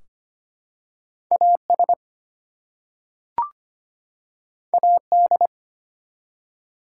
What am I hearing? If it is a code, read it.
Morse code, heard as a keyed tone, about 25 wpm, 700 Hz. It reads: AS AD
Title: Two Letter Words - 25wpm